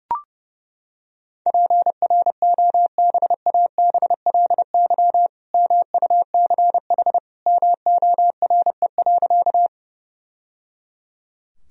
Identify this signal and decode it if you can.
Morse code, heard as a keyed tone, about 30 wpm, 700 Hz. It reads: PROBABLY MUCH MORE.
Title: Sentences from Top 1000 Words - Review Part 4 - 30wpm